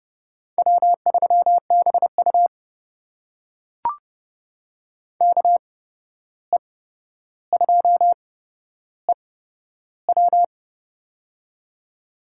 Morse code, heard as a keyed tone, about 30 wpm, spaced as for 12 wpm, 700 Hz. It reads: W3BU KE2EW
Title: USA Call Signs - 8x Character Spacing - 30wpm